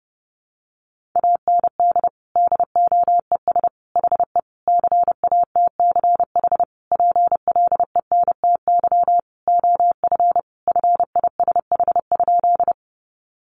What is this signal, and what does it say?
Morse code, heard as a keyed tone, about 30 wpm, 700 Hz. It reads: AND DOES HE CATCH PLENTY OF FISH?
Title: Sentences from Top 1500 Words 30wpm